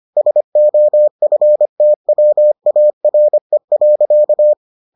Morse code, heard as a keyed tone, about 25 wpm, 600 Hz. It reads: SOFTWARE.